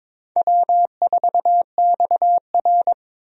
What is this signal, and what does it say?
Morse code, heard as a keyed tone, about 22 wpm, 700 Hz. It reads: W4XR
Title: USA Call Signs - 4 Characters - 22wpm